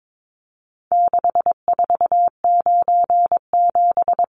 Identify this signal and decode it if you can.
Morse code, heard as a keyed tone, about 22 wpm, 700 Hz. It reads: 6497
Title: Numbers - 4-Digit 22wpm